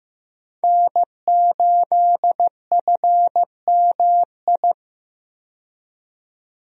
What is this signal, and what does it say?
Morse code, heard as a keyed tone, about 15 wpm, 700 Hz. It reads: N8FMI
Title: USA Call Signs - 5 Characters - 15wpm